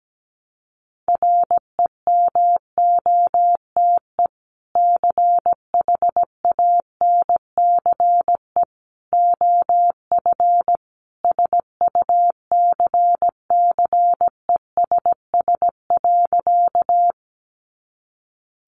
Morse code, heard as a keyed tone, about 17 wpm, 700 Hz. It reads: REMOTE CHANCE OF SUCCESS.
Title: Sentences from Top 1300 Words 17wpm